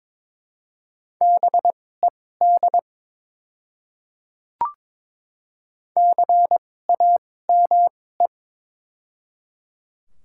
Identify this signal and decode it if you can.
Morse code, heard as a keyed tone, about 22 wpm, spaced as for 16 wpm, 700 Hz. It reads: BED CAME
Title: 2x Character Spacing - Top 500 Words - 22wpm